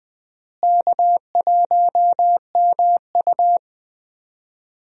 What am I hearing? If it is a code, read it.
Morse code, heard as a keyed tone, about 21 wpm, 700 Hz. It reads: K1MU
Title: USA Call Signs - 4 Characters - Repeated 3x - 20wpm